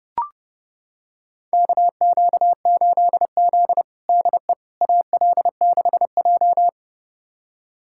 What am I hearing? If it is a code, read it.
Morse code, heard as a keyed tone, about 32 wpm, 700 Hz. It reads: KQ8Z DE AL6J
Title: USA Callsign DE Callsign - 4 Characters - 30wpm